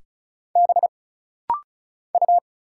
Morse code, heard as a keyed tone, about 36 wpm, 700 Hz. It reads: B U
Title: Single Letter-Number Alt - Rapid-Fire 35wpm